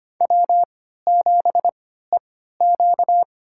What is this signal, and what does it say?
Morse code, heard as a keyed tone, about 25 wpm, spaced as for 14 wpm, 700 Hz. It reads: W7EQ
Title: USA Call Signs - 4 Characters - 3x Character Spacing - 25wpm